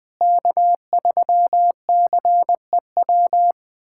Morse code, heard as a keyed tone, about 20 wpm, 700 Hz. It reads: K3CEW
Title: USA Callsign DE Callsign 20wpm